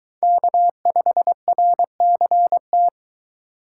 Morse code, heard as a keyed tone, about 22 wpm, 700 Hz. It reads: K5RCT